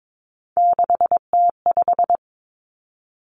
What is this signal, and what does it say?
Morse code, heard as a keyed tone, about 22 wpm, 700 Hz. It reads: 6T5
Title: Three Challenging Letter-Number Combinations 22wpm